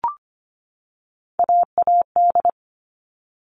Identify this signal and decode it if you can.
Morse code, heard as a keyed tone, about 25 wpm, 700 Hz. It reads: AAD
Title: All Cut Numbers - 3-Digit 25wpm